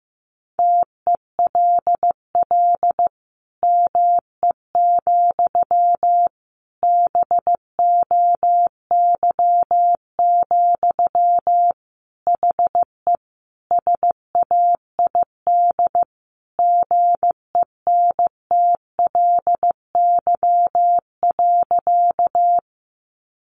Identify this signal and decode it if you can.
Morse code, heard as a keyed tone, about 15 wpm, 700 Hz. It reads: TELL ME, BOY, HE SAID GENTLY.
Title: Sentences from Top 1500 Words 15wpm